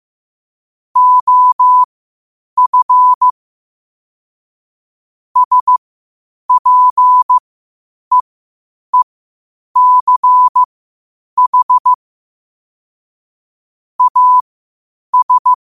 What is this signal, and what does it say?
Morse code, heard as a keyed tone, about 15 wpm, spaced as for 8 wpm, 1000 Hz. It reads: OF SPEECH AS
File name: qotdfiles / 8wpm